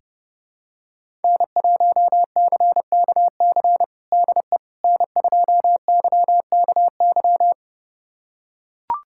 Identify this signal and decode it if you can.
Morse code, heard as a keyed tone, about 32 wpm, 700 Hz. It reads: N1CKC DE N2YKY
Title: USA Callsign DE Callsign - 5 Characters - 30wpm